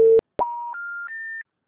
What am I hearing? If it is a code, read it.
Special information tone.